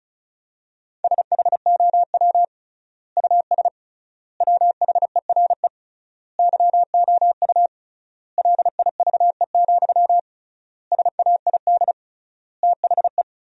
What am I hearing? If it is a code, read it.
Morse code, heard as a keyed tone, about 36 wpm, 700 Hz. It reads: SHOW US WHERE YOU LIVE, SAID THE
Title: 3x Word-Spacing - Sentences from Top 200 Words - 35wpm